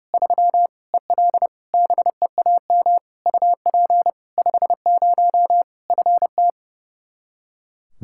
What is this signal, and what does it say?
Morse code, heard as a keyed tone, about 30 wpm, 700 Hz. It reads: 3 EL BEAM UP 50 FT
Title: K1USN SST QSO Practice 30wpm